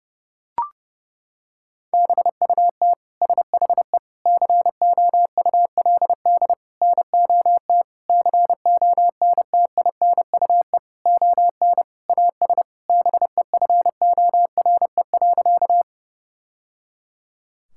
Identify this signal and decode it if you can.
Morse code, heard as a keyed tone, about 30 wpm, 700 Hz. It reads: BUT SHE COULD NOT CONTINUE ON AS BEFORE.
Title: Sentences from Top 1000 Words - Review Part 6 - 30wpm